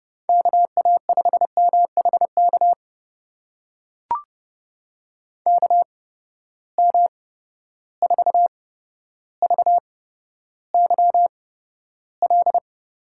Morse code, heard as a keyed tone, about 31 wpm, spaced as for 13 wpm, 700 Hz. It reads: KA5MHK KM4VYL
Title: USA Call Signs - 6 Characters - 8x Character Spacing - 30wpm